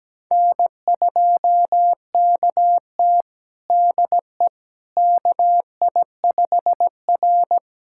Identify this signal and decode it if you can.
Morse code, heard as a keyed tone, about 17 wpm, 700 Hz. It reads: N2KT DE KI5R
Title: USA Callsign DE Callsign - 4 Characters - 17wpm